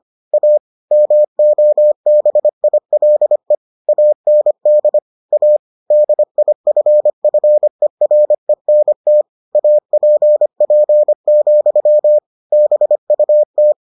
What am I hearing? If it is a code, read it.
Morse code, heard as a keyed tone, about 25 wpm, 600 Hz. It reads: A MOBILE AND A DIFFERENT APP, BUT